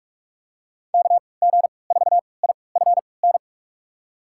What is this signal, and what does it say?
Morse code, heard as a keyed tone, about 46 wpm, spaced as for 25 wpm, 700 Hz. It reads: KG4IFN
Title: USA Call Signs - 6 Characters - 3x Character Spacing - 45wpm